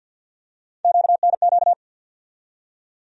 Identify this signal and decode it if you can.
Morse code, heard as a keyed tone, about 52 wpm, 700 Hz. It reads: QNQ
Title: USA Call Sign Postfixes - 3 Letters 50wpm